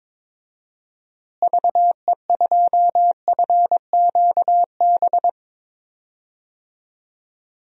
Morse code, heard as a keyed tone, about 22 wpm, 700 Hz. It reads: VE2FQB
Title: DXCC-100 Call Signs - 6 Characters - Repeated 2x - 22wpm